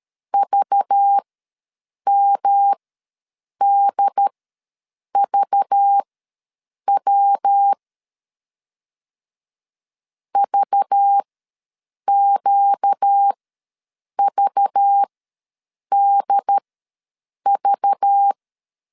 Morse code, heard as a keyed tone, about 13 wpm, spaced as for 7 wpm, 800 Hz. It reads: VMDVW VQVDV
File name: LEZIONI / 5 LBVC / 5 LBV ws 12wpm 3015